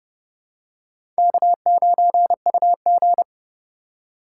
Morse code, heard as a keyed tone, about 30 wpm, 700 Hz. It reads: K9UG
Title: USA Call Signs - 4 Characters - 4x Character Spacing - 30wpm